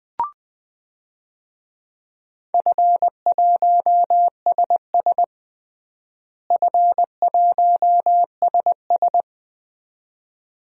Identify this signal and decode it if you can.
Morse code, heard as a keyed tone, about 20 wpm, 700 Hz. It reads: F1SS F1SS
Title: DXCC-100 Call Signs - 4 Characters - Repeated 2x - 20wpm